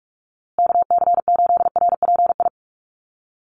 Morse code, heard as a keyed tone, about 44 wpm, 700 Hz. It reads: KC8RPI
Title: USA Call Signs Repeated 2x 45wpm